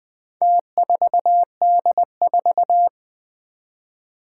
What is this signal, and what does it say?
Morse code, heard as a keyed tone, about 20 wpm, 700 Hz. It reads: T4D4
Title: Four Challenging Letter-Number Combinations 20wpm